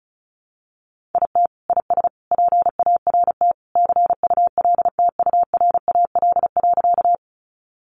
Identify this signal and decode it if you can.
Morse code, heard as a keyed tone, about 34 wpm, 700 Hz. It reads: IT IS PART CULTURAL.
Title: Sentences from Top 1500 Words 35wpm